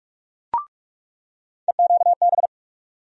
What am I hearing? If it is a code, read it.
Morse code, heard as a keyed tone, about 46 wpm, 700 Hz. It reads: EQB
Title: USA Call Sign Postfixes - 3 Letters 45wpm